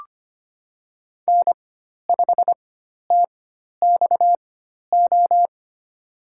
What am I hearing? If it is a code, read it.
Morse code, heard as a keyed tone, about 25 wpm, spaced as for 12 wpm, 700 Hz. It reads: N5TXO
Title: USA Call Signs - 5 Characters - 4x Character Spacing - 25wpm